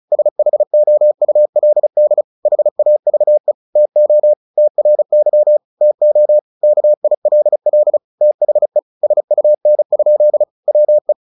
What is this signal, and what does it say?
Morse code, heard as a keyed tone, about 34 wpm, 600 Hz. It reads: SHOULD HAVE TO TRY TO KILL THE SUN? WE
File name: the old man and the sea 2 35wpm